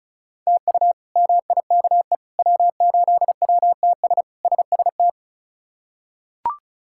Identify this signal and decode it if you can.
Morse code, heard as a keyed tone, about 36 wpm, 700 Hz. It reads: TU MIKE W8WTS SST
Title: K1USN SST QSO Practice - 35wpm